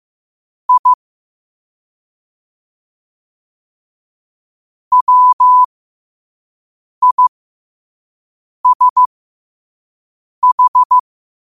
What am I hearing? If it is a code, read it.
Morse code, heard as a keyed tone, about 15 wpm, spaced as for 5 wpm, 1000 Hz. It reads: I WISH
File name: qotdfiles / 5wpm